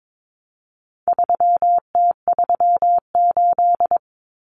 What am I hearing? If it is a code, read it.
Morse code, heard as a keyed tone, about 22 wpm, 700 Hz. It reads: 3T38
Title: Common Cut Numbers - 4-Digit 22wpm